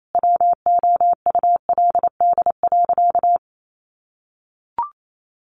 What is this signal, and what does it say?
Morse code, heard as a keyed tone, about 28 wpm, 700 Hz. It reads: WOULD.
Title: Sentences from Top 1500 Words 28wpm